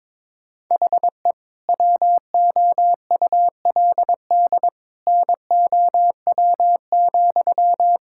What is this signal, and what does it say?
Morse code, heard as a keyed tone, about 22 wpm, 700 Hz. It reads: HE WOULD NOW,
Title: Sentences from Top 1000 Words - Review Part 5 - 22wpm